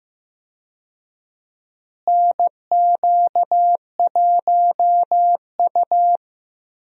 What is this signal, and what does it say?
Morse code, heard as a keyed tone, about 15 wpm, 700 Hz. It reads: NQ1U
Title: USA Call Signs - 4 Characters - Repeated 2x - 15wpm